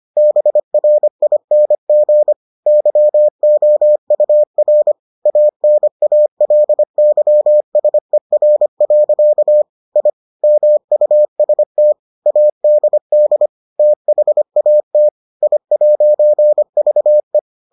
Morse code, heard as a keyed tone, about 25 wpm, 600 Hz. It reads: BRING YOUR ANALYSER. I MUST ADD THAT I'VE